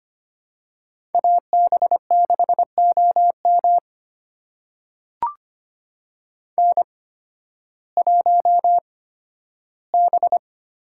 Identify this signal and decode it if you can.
Morse code, heard as a keyed tone, about 25 wpm, spaced as for 13 wpm, 700 Hz. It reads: AB6OM N1B
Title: USA Call Signs - 5 Characters - 8x Character Spacing - 25wpm